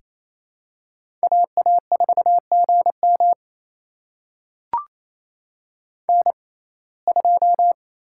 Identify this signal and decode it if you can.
Morse code, heard as a keyed tone, about 28 wpm, spaced as for 20 wpm, 700 Hz. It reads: AA4GM N2
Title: USA Call Signs - 5 Characters - 6x Character Spacing - 28wpm